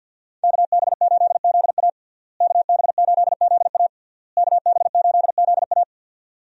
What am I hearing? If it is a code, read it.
Morse code, heard as a keyed tone, about 48 wpm, 700 Hz. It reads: KB8ZA KB8ZA KB8ZA
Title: USA Call Signs - 5 Characters - Repeated 3x - 50wpm